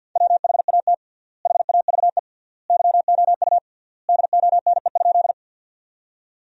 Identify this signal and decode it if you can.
Morse code, heard as a keyed tone, about 48 wpm, 700 Hz. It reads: WHAT HAVE YOU DONE?
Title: 3x Word-Spacing - Sentences from Top 500 Words - 50wpm